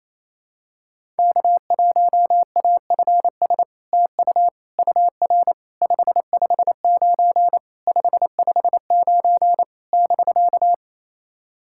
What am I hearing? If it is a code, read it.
Morse code, heard as a keyed tone, about 28 wpm, 700 Hz. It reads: K1AFS TU UR 559 559 <BK>
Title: SOTA QSO Practice 28wpm